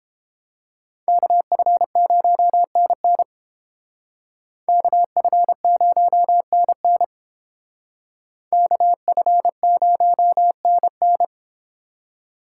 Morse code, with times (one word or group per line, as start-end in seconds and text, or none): 1.1-3.2 KF0NN
4.7-7.0 KF0NN
8.5-11.2 KF0NN